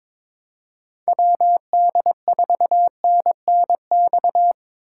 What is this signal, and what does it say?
Morse code, heard as a keyed tone, about 23 wpm, 700 Hz. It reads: WD4NNX
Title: USA Call Signs - 6 Characters - 2x Character Spacing - 22wpm